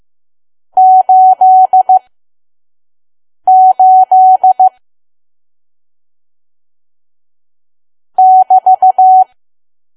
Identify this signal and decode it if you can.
Morse code, heard as a keyed tone, about 15 wpm, spaced as for 5 wpm, 750 Hz. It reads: 88 <BT>
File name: Archive / 5 WPM files / PM